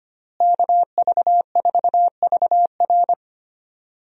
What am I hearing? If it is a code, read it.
Morse code, heard as a keyed tone, about 25 wpm, 700 Hz. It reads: KV4VR
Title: USA Call Signs - 5 Characters - Repeated 3x - 25wpm